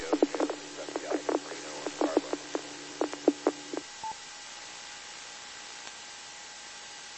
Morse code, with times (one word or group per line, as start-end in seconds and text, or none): none